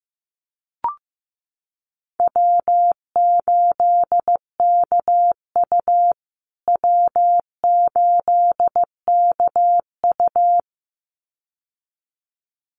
Morse code, heard as a keyed tone, about 15 wpm, 700 Hz. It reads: W8KU W8KU
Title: USA Call Signs Repeated 2x 15wpm